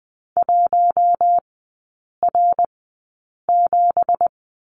Morse code, heard as a keyed tone, about 20 wpm, spaced as for 8 wpm, 700 Hz. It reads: 1R7
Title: 178 Lesson 29e - Slash - Three Characters 20wpm